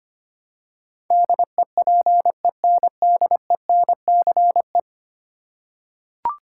Morse code, heard as a keyed tone, about 25 wpm, 700 Hz. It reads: DEPENDENCE
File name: Ten Letter Words - 25wpm